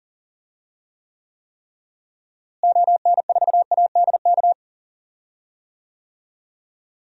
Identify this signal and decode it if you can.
Morse code, heard as a keyed tone, about 40 wpm, 700 Hz. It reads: ON4ADK
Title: DXCC-100 Call Signs - 6 Characters - Repeated 3x - 40wpm